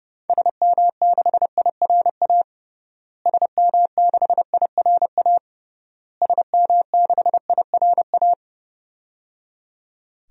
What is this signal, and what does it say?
Morse code, heard as a keyed tone, about 30 wpm, 700 Hz. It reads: SM6IRA SM6IRA SM6IRA
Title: DXCC-100 Call Signs - 6 Characters - Repeated 3x - 30wpm